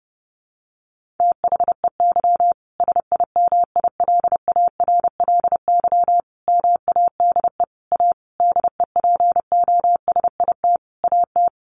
Morse code, heard as a keyed tone, about 30 wpm, 700 Hz. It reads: THEY SIMILARLY MADE A DEPOSIT AT
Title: Sentences from Top 1400 Words 30wpm